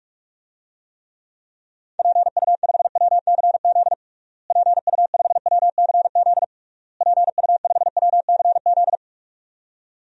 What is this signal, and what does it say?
Morse code, heard as a keyed tone, about 41 wpm, 700 Hz. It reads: PU5WCZ PU5WCZ PU5WCZ
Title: DXCC-100 Call Signs - 6 Characters - Repeated 3x - 45wpm